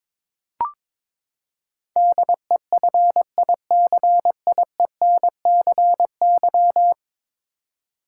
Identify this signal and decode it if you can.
Morse code, heard as a keyed tone, about 22 wpm, 700 Hz. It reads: DEFICIENCY
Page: morse-code-ninja-practice-22wpm